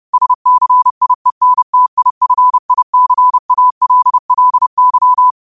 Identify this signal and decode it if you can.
Morse code, heard as a keyed tone, about 30 wpm, 1000 Hz. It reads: SCIENTIFICALLY
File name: 30wpm